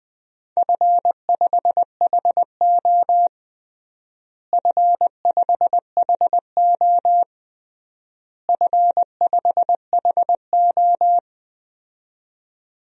Morse code, heard as a keyed tone, about 20 wpm, 700 Hz. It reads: F5HO F5HO F5HO
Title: DXCC-100 Call Signs - 4 Characters - Repeated 3x - 20wpm